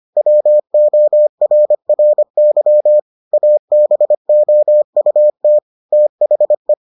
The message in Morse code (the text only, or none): WORRY ABOUT THE